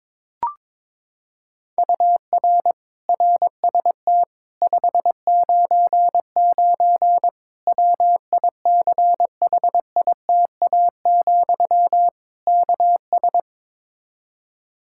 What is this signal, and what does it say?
Morse code, heard as a keyed tone, about 21 wpm, 700 Hz. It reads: UR RST 599 WICHITA, KS
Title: Practice QSO - Exchange 1 - Beginning - 22wpm